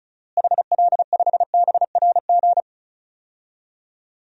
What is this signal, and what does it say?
Morse code, heard as a keyed tone, about 36 wpm, 700 Hz. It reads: HL5BRG
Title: DXCC-100 Call Signs - 6 Characters - Repeated 3x - 35wpm